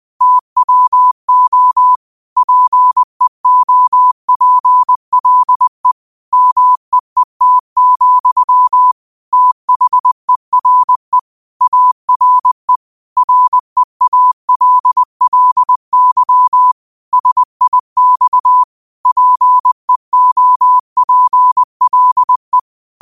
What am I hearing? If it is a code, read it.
Morse code, heard as a keyed tone, about 20 wpm, 1000 Hz. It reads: TWO PEOPLE MEET, THERE ARE REALLY SIX PEOPLE
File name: qotdfiles / 20wpm